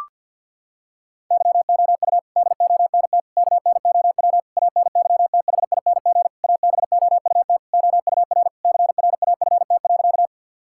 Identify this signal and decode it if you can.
Morse code, heard as a keyed tone, about 52 wpm, 700 Hz. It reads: YOU DONT KNOW ANYTHING ABOUT OUR CRAFT.
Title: Sentences from Top 1000 Words - Review Part 6 - 50wpm